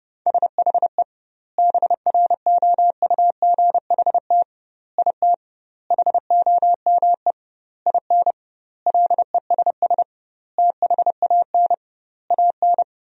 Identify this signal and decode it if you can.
Morse code, heard as a keyed tone, about 30 wpm, 700 Hz. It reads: SHE BROUGHT IT HOME IN LESS THAN AN
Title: 2x Word-Spacing - Sentences from Top 500 Words - 30wpm